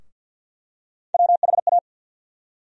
Morse code, heard as a keyed tone, about 52 wpm, 700 Hz. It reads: WHA
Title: USA Call Sign Postfixes - 3 Letters 50wpm